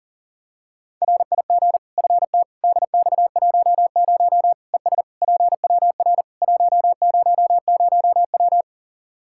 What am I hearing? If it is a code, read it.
Morse code, heard as a keyed tone, about 40 wpm, 700 Hz. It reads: RIG FT DX10 ES PWR 100W
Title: Practice QSO - Exchange 2 - Info - 40wpm